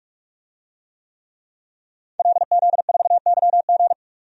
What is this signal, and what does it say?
Morse code, heard as a keyed tone, about 46 wpm, 700 Hz. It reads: LZ4YG
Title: DXCC-100 Call Signs - 5 Characters - 45wpm